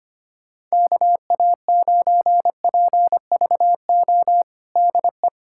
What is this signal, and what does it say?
Morse code, heard as a keyed tone, about 25 wpm, 700 Hz. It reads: KA9PVO DE
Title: USA Callsign DE Callsign - 6 Characters - 25wpm